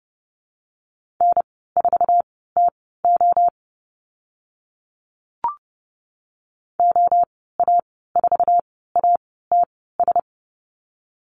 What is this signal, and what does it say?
Morse code, heard as a keyed tone, about 30 wpm, spaced as for 17 wpm, 700 Hz. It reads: N4TO OA4ATS
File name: Lesson 14g - Number 2 - Call Signs 30wpm